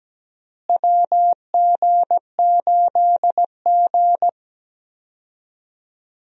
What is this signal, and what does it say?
Morse code, heard as a keyed tone, about 17 wpm, 700 Hz. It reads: WG8G
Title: USA Call Signs - 4 Characters - Repeated 2x - 17wpm